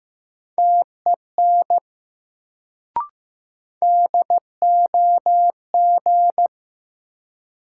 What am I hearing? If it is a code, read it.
Morse code, heard as a keyed tone, about 16 wpm, 700 Hz. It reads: TEN DOG
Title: Sentences from Top 400 Words - New Word Drill - 15wpm